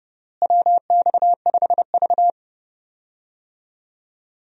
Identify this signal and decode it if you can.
Morse code, heard as a keyed tone, about 31 wpm, 700 Hz. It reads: WX5V
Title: USA Call Signs - 4 Characters - Repeated 2x - 30wpm